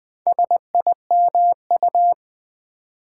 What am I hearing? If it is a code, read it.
Morse code, heard as a keyed tone, about 20 wpm, 700 Hz. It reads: SIMU
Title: Build A Word - 2 Letters At A Time - 8-Letter Words - 20wpm